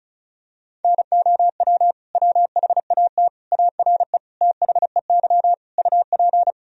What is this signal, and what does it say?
Morse code, heard as a keyed tone, about 36 wpm, 700 Hz. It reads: NOW WHAT ARE THEY UP
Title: Sentences from Top 1000 Words - Review Part 1 - 35wpm